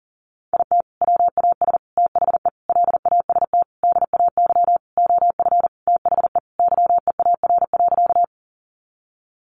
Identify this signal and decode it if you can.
Morse code, heard as a keyed tone, about 37 wpm, 700 Hz. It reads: IT WAS THE LAST DAY OF THE YEAR.
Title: Sentences from Top 300 Words 40wpm